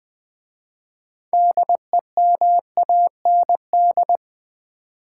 Morse code, heard as a keyed tone, about 20 wpm, 700 Hz. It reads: DEMAND
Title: 6 Letter Words - First 3 Letters Given - 20wpm